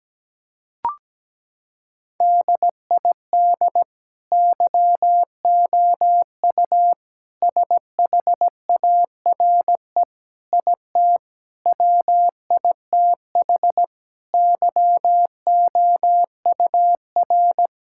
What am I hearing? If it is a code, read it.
Morse code, heard as a keyed tone, about 17 wpm, 700 Hz. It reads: DID YOU SHARE IT WITH YOUR
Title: Sentences from Top 1000 Words - Review Part 6 - 17wpm